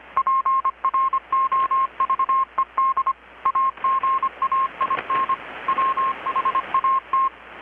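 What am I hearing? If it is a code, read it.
Morse code, heard as a keyed tone, about 25 wpm, 1070 Hz. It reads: PROVED AGAIN WHAT
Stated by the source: Morse from KPH's Night of Nights XX maritime radiotelegraphy event, received in Chiapas, Mexico about 3,800 km away on a Kenwood R-600 with a grounded 10 m long-wire antenna.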